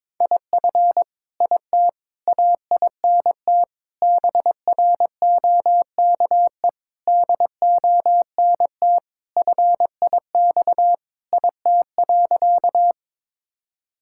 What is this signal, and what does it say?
Morse code, heard as a keyed tone, about 22 wpm, 700 Hz. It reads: IF IT AINT BROKE DONT FIX IT.
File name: Sentences with 7-Word Phrases - 22wpm